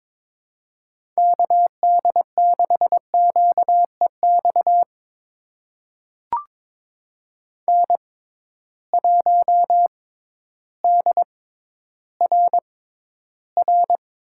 Morse code, heard as a keyed tone, about 22 wpm, spaced as for 12 wpm, 700 Hz. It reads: KD6QEX N1DRR
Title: USA Call Signs - 6x Character Spacing - 22wpm